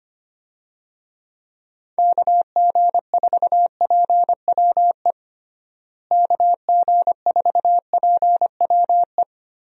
Morse code, heard as a keyed tone, about 25 wpm, 700 Hz. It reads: KG4PWE KG4PWE
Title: USA Call Signs - 6 Characters - Repeated 3x - 25wpm